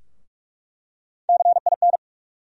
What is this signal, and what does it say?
Morse code, heard as a keyed tone, about 46 wpm, 700 Hz. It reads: CIN
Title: USA Call Sign Postfixes - 3 Letters 45wpm